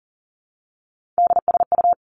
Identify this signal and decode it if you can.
Morse code, heard as a keyed tone, about 41 wpm, 690 Hz. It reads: DSU
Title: Three Challenging Letter-Number Combinations 40wpm